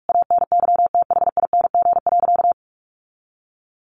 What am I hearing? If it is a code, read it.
Morse code, heard as a keyed tone, about 44 wpm, 700 Hz. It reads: ANYTHING.
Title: Sentences from Top 1000 Words 45wpm